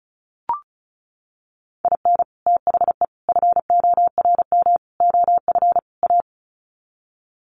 Morse code, heard as a keyed tone, about 34 wpm, 700 Hz. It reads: IN THE FORM OF A
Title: Sets of 5 Words 35wpm